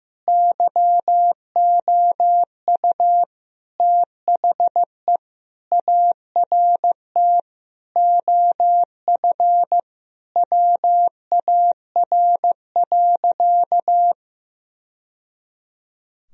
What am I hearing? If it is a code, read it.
Morse code, heard as a keyed tone, about 15 wpm, 700 Hz. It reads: YOU THE ART OF WAR.
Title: Sentences from Top 1000 Words - Review Part 3 - 15wpm